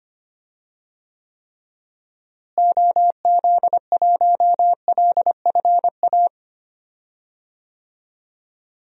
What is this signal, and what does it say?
Morse code, heard as a keyed tone, about 25 wpm, 700 Hz. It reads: OZ1LFA
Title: DXCC-100 Call Signs - 6 Characters - Repeated 3x - 25wpm